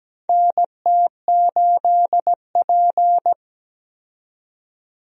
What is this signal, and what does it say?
Morse code, heard as a keyed tone, about 17 wpm, 700 Hz. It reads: NT8P
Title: USA Call Signs - 4 Characters - 17wpm